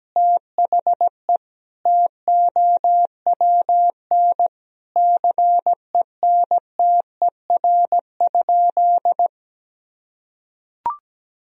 Morse code, heard as a keyed tone, about 17 wpm, 700 Hz. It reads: THE TOWN CENTER?
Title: Sentences from Top 1000 Words - Review Part 2 - 17wpm